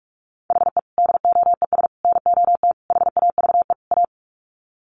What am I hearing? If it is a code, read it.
Morse code, heard as a keyed tone, about 44 wpm, 700 Hz. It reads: HE DOES NOT HAVE A
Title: Sets of 5 Words Encore 45wpm